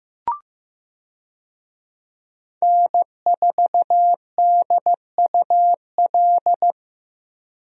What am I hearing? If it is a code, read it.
Morse code, heard as a keyed tone, about 15 wpm, 700 Hz. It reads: N4DUL
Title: USA Call Signs - 5 Characters - Repeated 3x - 15wpm